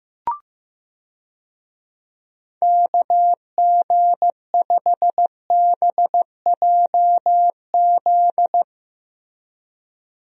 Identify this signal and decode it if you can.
Morse code, heard as a keyed tone, about 15 wpm, 700 Hz. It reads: KG5BJZ
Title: USA Call Signs - 6 Characters - Repeated 2x - 15wpm